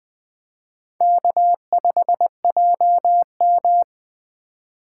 Morse code, heard as a keyed tone, about 20 wpm, 700 Hz. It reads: K5JM
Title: Speed Racing - USA 1x2 Call Signs 17wpm